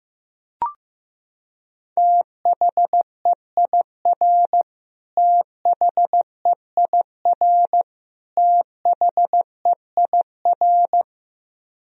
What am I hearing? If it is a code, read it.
Morse code, heard as a keyed tone, about 15 wpm, 700 Hz. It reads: THEIR THEIR THEIR
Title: Top 100 Words Repeated 15wpm